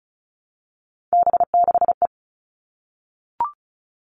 Morse code, heard as a keyed tone, about 34 wpm, 700 Hz. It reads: B6E
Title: All Cut Numbers - 3-Digit 35wpm